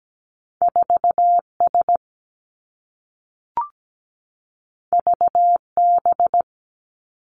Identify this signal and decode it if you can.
Morse code, heard as a keyed tone, about 17 wpm, 700 Hz. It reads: 4S VB
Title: Two Challenging Letter-Number Combinations 17wpm